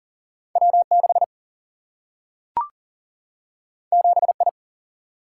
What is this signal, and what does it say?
Morse code, heard as a keyed tone, about 41 wpm, 700 Hz. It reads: W6 7I
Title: Single-Letter Single-Number 40wpm